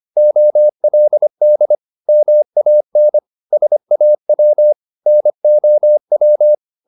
Morse code, heard as a keyed tone, about 25 wpm, 600 Hz. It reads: OLD MAN SAW NOW
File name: the old man and the sea part 1 25wpm